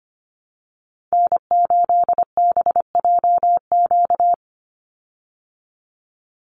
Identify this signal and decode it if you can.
Morse code, heard as a keyed tone, about 25 wpm, 700 Hz. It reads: N8BJQ
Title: Call Signs Repeated 3x - 25wpm